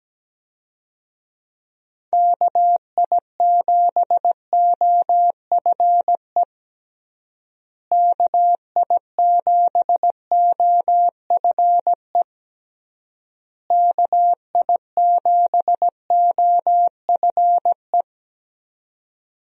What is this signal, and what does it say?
Morse code, heard as a keyed tone, about 17 wpm, 700 Hz. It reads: KI7OFE KI7OFE KI7OFE
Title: USA Call Signs - 6 Characters - Repeated 3x - 17wpm